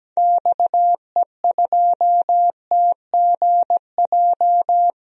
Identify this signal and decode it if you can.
Morse code, heard as a keyed tone, about 17 wpm, 700 Hz. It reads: XE2TGJ
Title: DXCC-100 Call Signs - 6 Characters - Repeated 2x - 17wpm